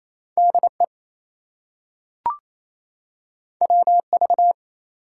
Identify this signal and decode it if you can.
Morse code, heard as a keyed tone, about 28 wpm, 700 Hz. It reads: DE WV
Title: US States and Canadian Provinces 28wpm